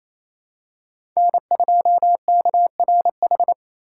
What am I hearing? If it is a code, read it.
Morse code, heard as a keyed tone, about 28 wpm, 700 Hz. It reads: N2KRH
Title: USA Call Signs - 6x Character Spacing - 28wpm